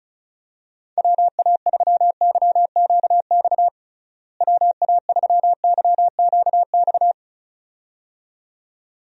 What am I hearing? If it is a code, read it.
Morse code, heard as a keyed tone, about 36 wpm, 700 Hz. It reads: WA3YQX WA3YQX
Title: USA Call Signs - 6 Characters - Repeated 2x - 35wpm